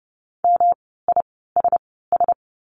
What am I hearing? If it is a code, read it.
Morse code, heard as a keyed tone, about 30 wpm, spaced as for 17 wpm, 700 Hz. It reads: MISS
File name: course / 30:10wpm / 108 Lesson 18f - Letter m - Words 30wpm